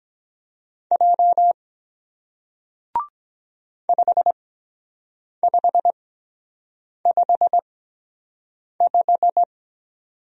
Morse code, heard as a keyed tone, about 23 wpm, spaced as for 7 wpm, 700 Hz. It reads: J 5555